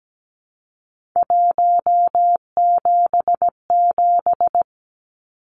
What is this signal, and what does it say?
Morse code, heard as a keyed tone, about 17 wpm, 700 Hz. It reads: 177
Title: Numbers - 3-Digit 17wpm